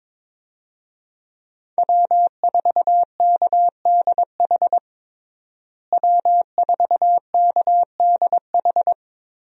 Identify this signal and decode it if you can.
Morse code, heard as a keyed tone, about 22 wpm, 700 Hz. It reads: W4KDH W4KDH
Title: USA Call Signs - 5 Characters - Repeated 2x - 22wpm